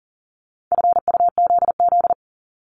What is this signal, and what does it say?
Morse code, heard as a keyed tone, about 42 wpm, 700 Hz. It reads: FUZZ
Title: Multi-letter Phonemes 40wpm